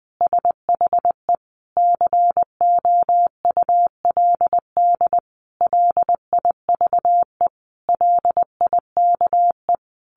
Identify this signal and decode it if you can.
Morse code, heard as a keyed tone, about 19 wpm, 700 Hz. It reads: SHE COULD LIVE LIKE
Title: Sentences from Top 1400 Words 20wpm